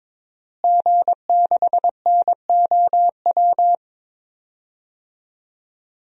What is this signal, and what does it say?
Morse code, heard as a keyed tone, about 22 wpm, 700 Hz. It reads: G6NOW
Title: DXCC-100 Call Signs - 5 Characters - Repeated 3x - 22wpm